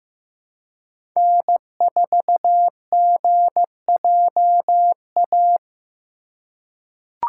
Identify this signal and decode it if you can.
Morse code, heard as a keyed tone, about 15 wpm, 700 Hz. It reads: N4GJA
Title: USA Call Signs - 5 Characters - 2x Character Spacing - 15wpm